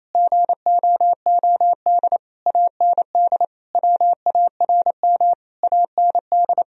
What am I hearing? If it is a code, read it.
Morse code, heard as a keyed tone, about 28 wpm, 700 Hz. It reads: GOOD AND WARM AND
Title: Sentences from Top 1000 Words - Review Part 3 - 28wpm